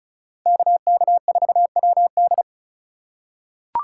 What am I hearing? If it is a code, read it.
Morse code, heard as a keyed tone, about 36 wpm, 700 Hz. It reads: KK4WD
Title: USA Call Signs - 5 Characters - 2x Character Spacing - 35wpm